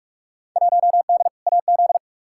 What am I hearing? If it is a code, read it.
Morse code, heard as a keyed tone, about 43 wpm, 700 Hz. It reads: 1D AZ
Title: Field Day Exchanges - 45wpm